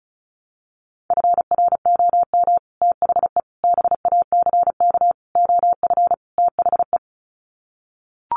Morse code, heard as a keyed tone, about 34 wpm, 700 Hz. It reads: FROM THE BACK OF THE
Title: Sets of 5 Words Encore 35wpm